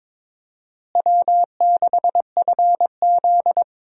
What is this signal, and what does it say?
Morse code, heard as a keyed tone, about 22 wpm, 700 Hz. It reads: W6FZ
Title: USA Call Signs - 4 Characters - Repeated 2x - 22wpm